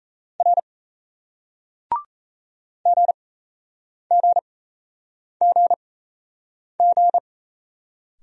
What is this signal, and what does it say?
Morse code, heard as a keyed tone, about 38 wpm, spaced as for 8 wpm, 700 Hz. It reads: R GGGG